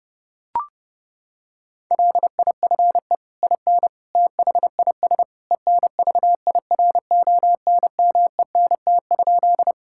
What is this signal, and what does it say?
Morse code, heard as a keyed tone, about 30 wpm, 700 Hz. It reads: LIFE IN THIS ENVIRONMENT?
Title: Sentences from Top 1000 Words - Review Part 5 - 30wpm